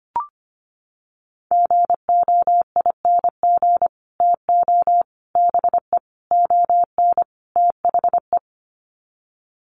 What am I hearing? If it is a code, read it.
Morse code, heard as a keyed tone, about 25 wpm, 700 Hz. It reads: GOING TO BE ON THE
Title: Sets of 5 Words Encore2 25wpm